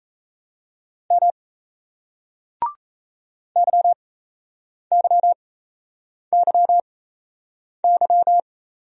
Morse code, heard as a keyed tone, about 38 wpm, spaced as for 8 wpm, 700 Hz. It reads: M YYYY